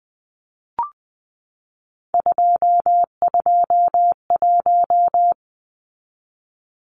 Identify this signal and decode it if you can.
Morse code, heard as a keyed tone, about 20 wpm, 700 Hz. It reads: 221
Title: Numbers - 3-Digit 20wpm